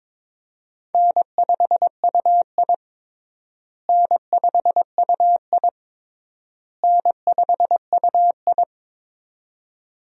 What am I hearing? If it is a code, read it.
Morse code, heard as a keyed tone, about 21 wpm, 700 Hz. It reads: N5UI N5UI N5UI
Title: USA Call Signs - 4 Characters - Repeated 3x - 22wpm